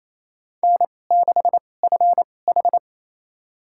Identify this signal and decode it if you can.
Morse code, heard as a keyed tone, about 28 wpm, spaced as for 20 wpm, 700 Hz. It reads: N6FH
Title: USA Call Signs - 2x Character Spacing - 28wpm